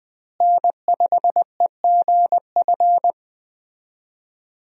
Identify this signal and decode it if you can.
Morse code, heard as a keyed tone, about 20 wpm, 700 Hz. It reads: N5EGF
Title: USA Call Signs - 5 Characters - Repeated 2x - 20wpm